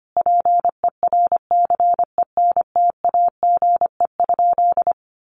Morse code, heard as a keyed tone, about 25 wpm, 700 Hz. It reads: PERCENTAGE?
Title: Sentences from Top 1300 Words 25wpm